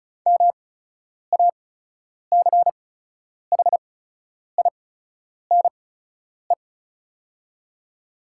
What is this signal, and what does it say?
Morse code, heard as a keyed tone, about 34 wpm, spaced as for 10 wpm, 690 Hz. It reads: MACHINE
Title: 8x Character Spacing - Top 500 Words - 35wpm